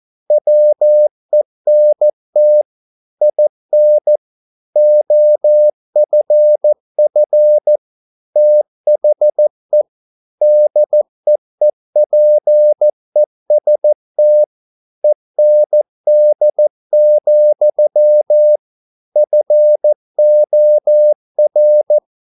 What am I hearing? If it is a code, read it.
Morse code, heard as a keyed tone, about 14 wpm, 600 Hz. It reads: WENT IN OFF THE DEEPEST END, FOR
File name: mp3 files / a dogs mistake - 14wpm